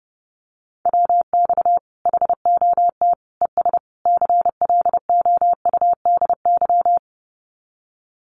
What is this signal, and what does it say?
Morse code, heard as a keyed tone, about 30 wpm, 700 Hz. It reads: WX HOT ES CLOUDY
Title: Common QSO Phrases 30wpm